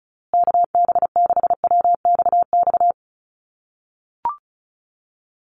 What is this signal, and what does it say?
Morse code, heard as a keyed tone, about 34 wpm, 700 Hz. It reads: KB6WXX
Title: USA Call Signs Repeated 2x 35wpm